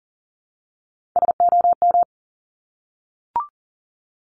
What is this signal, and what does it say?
Morse code, heard as a keyed tone, about 39 wpm, 700 Hz. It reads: SOM
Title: Top Three Letter Combinations in Words 40wpm